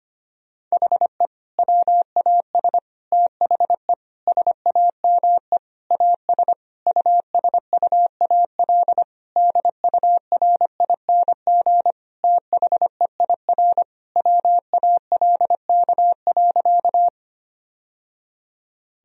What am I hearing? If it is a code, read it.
Morse code, heard as a keyed tone, about 25 wpm, 700 Hz. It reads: HE WAS THE SAME AS USUAL DURING THEIR WALK.
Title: Sentences from Top 1000 Words - Review Part 2 - 25wpm